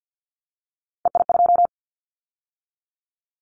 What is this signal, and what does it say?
Morse code, heard as a keyed tone, about 50 wpm, 700 Hz. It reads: EI2
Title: Callsign Prefixes for Top 50 DXCC - 50wpm